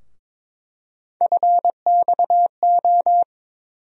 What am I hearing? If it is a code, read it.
Morse code, heard as a keyed tone, about 22 wpm, 700 Hz. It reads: FXO
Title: USA Call Sign Postfixes - 3 Letters 22wpm